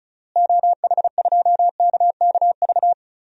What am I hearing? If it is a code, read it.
Morse code, heard as a keyed tone, about 36 wpm, 700 Hz. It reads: OH2KKV
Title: DXCC-100 Call Signs - 6 Characters - Repeated 3x - 35wpm